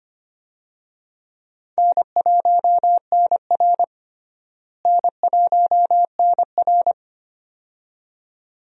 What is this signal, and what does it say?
Morse code, heard as a keyed tone, about 25 wpm, 700 Hz. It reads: N1NR N1NR
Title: USA Call Signs - 4 Characters - Repeated 2x - 25wpm